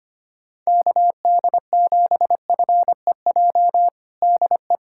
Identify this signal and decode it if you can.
Morse code, heard as a keyed tone, about 25 wpm, 700 Hz. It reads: KD7FEJ DE
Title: USA Callsign DE Callsign - 6 Characters - 25wpm